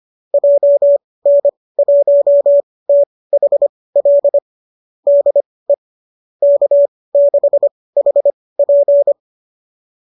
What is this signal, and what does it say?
Morse code, heard as a keyed tone, about 25 wpm, spaced as for 18 wpm, 590 Hz. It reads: JN1THL DE K6HP
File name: qso205 18